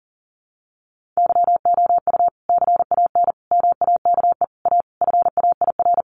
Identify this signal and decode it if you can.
Morse code, heard as a keyed tone, about 40 wpm, 700 Hz. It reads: YOU CAN MAKE A FAIR
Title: Sentences from Top 1400 Words 40wpm